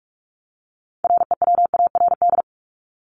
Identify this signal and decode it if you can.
Morse code, heard as a keyed tone, about 44 wpm, 700 Hz. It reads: REWARD
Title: Top 1751-2000 Words 45wpm